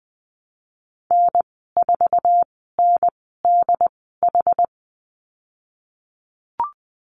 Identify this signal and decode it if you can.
Morse code, heard as a keyed tone, about 20 wpm, spaced as for 15 wpm, 700 Hz. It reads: N4NDH
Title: 186 Lesson 30g - Letter b - Call Signs 20wpm